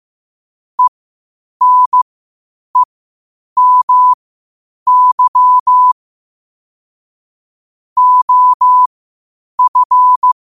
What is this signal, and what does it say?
Morse code, heard as a keyed tone, about 15 wpm, spaced as for 8 wpm, 1000 Hz. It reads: ENEMY OF
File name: qotdfiles / 8wpm